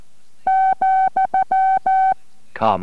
Morse code, heard as a keyed tone, about 14 wpm, spaced as for 10 wpm, 760 Hz. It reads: ,E